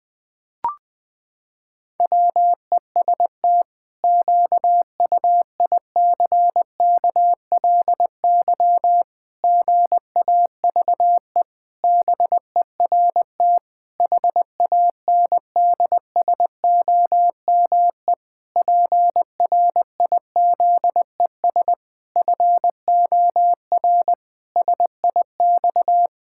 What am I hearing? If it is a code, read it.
Morse code, heard as a keyed tone, about 21 wpm, 700 Hz. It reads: WEST QUICKLY GAVE BERT HANDSOME PRIZES FOR SIX
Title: Pangram Challenge v2 - 20wpm